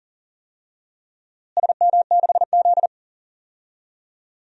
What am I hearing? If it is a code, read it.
Morse code, heard as a keyed tone, about 40 wpm, 700 Hz. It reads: SM6Z
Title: DXCC-100 Call Signs - 4 Characters - Repeated 2x - 40wpm